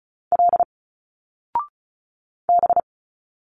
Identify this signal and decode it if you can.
Morse code, heard as a keyed tone, about 36 wpm, 700 Hz. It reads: L B
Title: Single Letters 35wpm